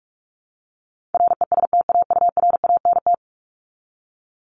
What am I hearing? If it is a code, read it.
Morse code, heard as a keyed tone, about 44 wpm, 700 Hz. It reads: RESTAURANT
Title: Top 1751-2000 Words 45wpm